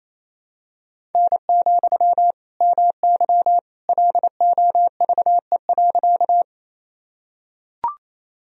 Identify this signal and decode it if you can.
Morse code, heard as a keyed tone, about 28 wpm, 700 Hz. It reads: N, MY LOVE.
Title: Sentences from Top 400 Words and Abbreviations 28wpm